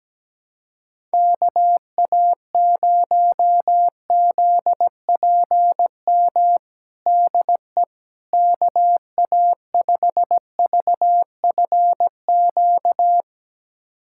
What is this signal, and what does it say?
Morse code, heard as a keyed tone, about 17 wpm, 700 Hz. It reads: KA0ZPM DE KA5VFQ
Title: USA Callsign DE Callsign - 6 Characters - 17wpm